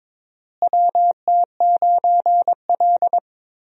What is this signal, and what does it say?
Morse code, heard as a keyed tone, about 22 wpm, 700 Hz. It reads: WT9L
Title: USA Call Signs - 4 Characters - Repeated 3x - 22wpm